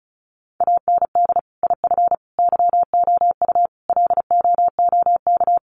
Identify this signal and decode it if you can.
Morse code, heard as a keyed tone, about 34 wpm, 700 Hz. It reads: AND IF YOU LOOK
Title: Sets of 5 Words Encore2 35wpm